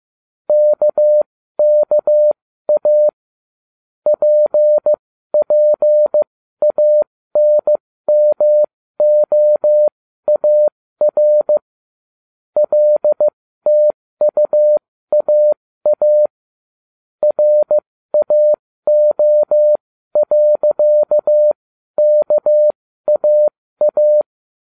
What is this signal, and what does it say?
Morse code, heard as a keyed tone, about 15 wpm, spaced as for 13 wpm, 600 Hz. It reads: KKA PPANMOAR LTUAA RAO.KAA